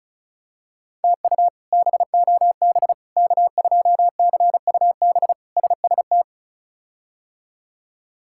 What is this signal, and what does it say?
Morse code, heard as a keyed tone, about 36 wpm, 700 Hz. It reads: TU BOB K2CUB SST
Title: K1USN SST QSO Practice - 35wpm